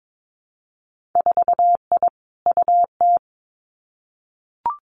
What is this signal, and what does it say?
Morse code, heard as a keyed tone, about 22 wpm, 700 Hz. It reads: 4I UT
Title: Winter Field Day Exchanges Encore 22wpm